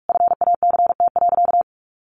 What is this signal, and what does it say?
Morse code, heard as a keyed tone, about 44 wpm, 700 Hz. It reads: FACT.
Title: Sentences from Top 600 Words 45wpm